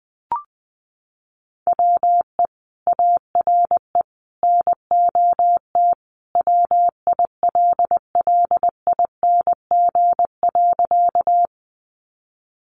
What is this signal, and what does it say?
Morse code, heard as a keyed tone, about 20 wpm, 700 Hz. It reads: WE ARE NOT WILLING.
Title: Sentences from Top 1000 Words 20wpm